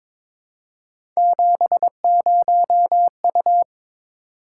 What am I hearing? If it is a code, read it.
Morse code, heard as a keyed tone, about 22 wpm, 700 Hz. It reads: 70U
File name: Three Letter-Number Combinations - 22wpm